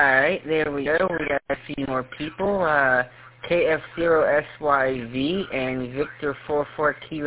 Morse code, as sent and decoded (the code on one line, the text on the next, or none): none